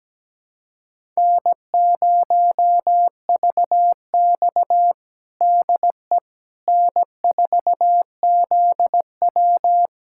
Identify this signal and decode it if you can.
Morse code, heard as a keyed tone, about 17 wpm, 700 Hz. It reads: N0VX DE N4ZW
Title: USA Callsign DE Callsign - 4 Characters - 17wpm